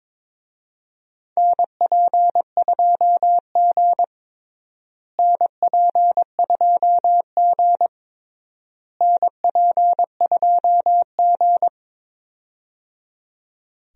Morse code, heard as a keyed tone, about 22 wpm, 700 Hz. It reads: NP2G NP2G NP2G
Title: USA Call Signs - 4 Characters - Repeated 3x - 22wpm